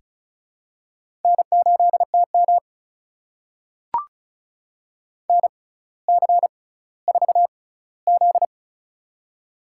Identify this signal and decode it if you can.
Morse code, heard as a keyed tone, about 36 wpm, spaced as for 18 wpm, 700 Hz. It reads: N8TM NC4Z
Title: USA Call Signs - 4 Characters - 6x Character Spacing - 35wpm